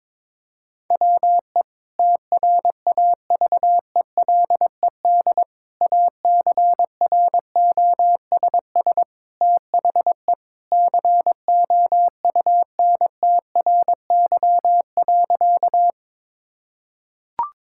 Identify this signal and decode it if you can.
Morse code, heard as a keyed tone, about 22 wpm, 700 Hz. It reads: WE TRAVELED ACROSS THE COUNTRY.
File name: practice / 22wpm / Sentences with 2-Word Phrases - 22wpm